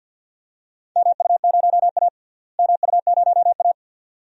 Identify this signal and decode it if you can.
Morse code, heard as a keyed tone, about 48 wpm, 700 Hz. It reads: MU0A MU0A
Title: DXCC-100 Call Signs - 4 Characters - Repeated 2x - 50wpm